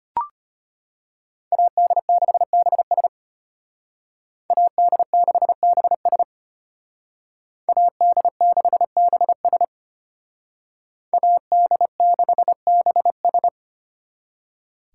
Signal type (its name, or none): Morse code